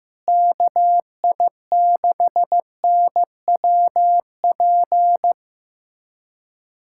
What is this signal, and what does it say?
Morse code, heard as a keyed tone, about 15 wpm, 700 Hz. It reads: KI6NWP
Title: USA Callsign DE Callsign - 6 Characters - 15wpm